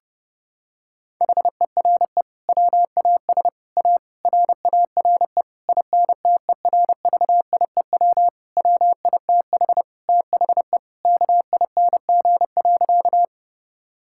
Morse code, heard as a keyed tone, about 30 wpm, 700 Hz. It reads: HERE WAS A RARE INTERVIEW WITH THE KING.
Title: Sentences from Top 1600 Words 30wpm